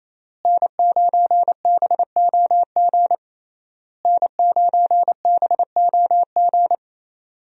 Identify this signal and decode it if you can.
Morse code, heard as a keyed tone, about 28 wpm, 700 Hz. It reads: N9BOG N9BOG
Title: USA Call Signs - 5 Characters - Repeated 3x - 28wpm